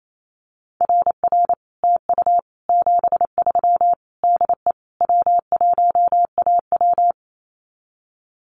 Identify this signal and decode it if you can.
Morse code, heard as a keyed tone, about 28 wpm, 700 Hz. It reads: RR TU 73 DE W1AW
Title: Common QSO Phrases Encore 28wpm